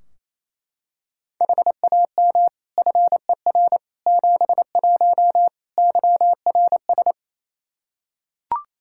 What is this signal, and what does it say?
Morse code, heard as a keyed tone, about 28 wpm, 700 Hz. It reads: HAM FER 71 YRS
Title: Practice QSO - Exchange 3 - More Info - 28wpm